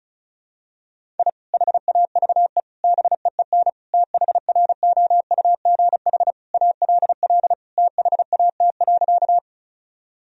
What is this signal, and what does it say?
Morse code, heard as a keyed tone, about 36 wpm, 700 Hz. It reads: I HAVE BEEN THROUGH ALL THAT.
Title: Sentences from Top 1000 Words - Review Part 1 - 35wpm